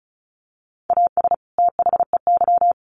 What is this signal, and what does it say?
Morse code, heard as a keyed tone, about 34 wpm, 700 Hz. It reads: AS THEY
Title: Sets of 2 Words 35wpm